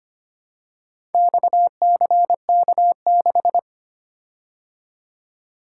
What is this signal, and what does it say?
Morse code, heard as a keyed tone, about 25 wpm, 700 Hz. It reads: XCK6
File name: practice / 25wpm / Four Letter-Number Combinations - 25wpm